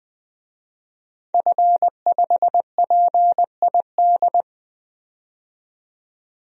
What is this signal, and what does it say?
Morse code, heard as a keyed tone, about 20 wpm, 700 Hz. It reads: F5PID
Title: DXCC-100 Call Signs - 5 Characters - 20wpm